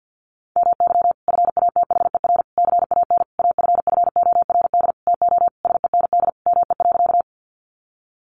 Morse code, heard as a keyed tone, about 50 wpm, 700 Hz. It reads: MY FATHER CAN AFFORD TO SEND ME.
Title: Sentences from Top 1500 Words 50wpm